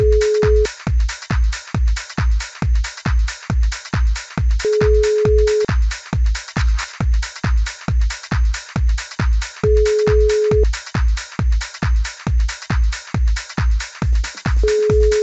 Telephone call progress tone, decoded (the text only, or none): ringback tone